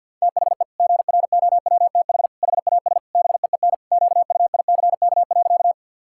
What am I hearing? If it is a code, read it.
Morse code, heard as a keyed tone, about 50 wpm, 700 Hz. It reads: THE GROWTH HAS BEEN QUICK.
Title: Sentences from Top 1000 Words - Review Part 5 - 50wpm